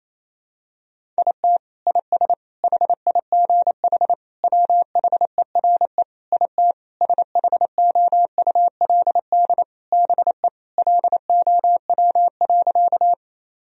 Morse code, heard as a keyed tone, about 28 wpm, 700 Hz. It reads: IT IS HIGH WHERE IT SHOULD BE LOW.
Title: Sentences from Top 1000 Words - Review Part 1 - 28wpm